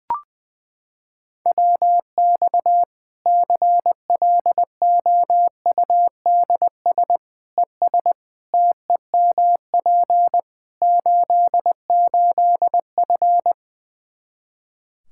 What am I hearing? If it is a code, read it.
Morse code, heard as a keyed tone, about 21 wpm, 700 Hz. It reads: WX CLOUDS ES TEMP 88F
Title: K1USN SST QSO Practice 20wpm